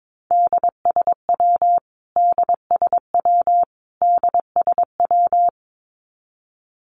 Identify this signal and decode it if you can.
Morse code, heard as a keyed tone, about 22 wpm, 700 Hz. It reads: DSW DSW DSW
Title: Abbreviations Repeated 3x - 22wpm